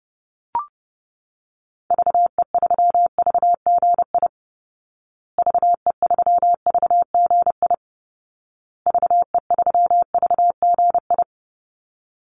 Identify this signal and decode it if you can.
Morse code, heard as a keyed tone, about 30 wpm, 700 Hz. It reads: VE3VGI VE3VGI VE3VGI
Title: Call Signs Repeated 2x - 30wpm